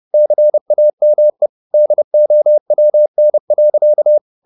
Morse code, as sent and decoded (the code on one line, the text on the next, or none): -.-. .- -- . / -.. --- .-- -. .-.-.-
CAME DOWN.